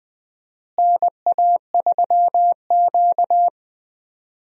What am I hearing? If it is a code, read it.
Morse code, heard as a keyed tone, about 20 wpm, 700 Hz. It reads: NA3Q
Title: USA Call Signs - 4 Characters - Repeated 2x - 20wpm